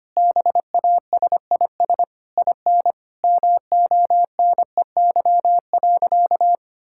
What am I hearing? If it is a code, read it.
Morse code, heard as a keyed tone, about 25 wpm, 700 Hz. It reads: BASIS IN MONEY.
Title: Sentences from Top 1000 Words - Review Part 5 - 25wpm